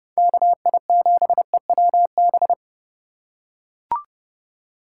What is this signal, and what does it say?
Morse code, heard as a keyed tone, about 29 wpm, 700 Hz. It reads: KI7EWB